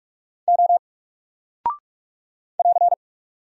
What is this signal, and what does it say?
Morse code, heard as a keyed tone, about 44 wpm, 700 Hz. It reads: O <AR>
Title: Single Character (Letter:Number:Punctuation:Prosign) 45wpm